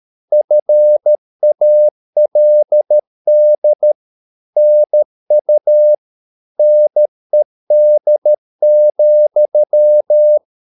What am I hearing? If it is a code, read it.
Morse code, heard as a keyed tone, about 13 wpm, 600 Hz. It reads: FALD NU NED,
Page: Ferie 13 wpm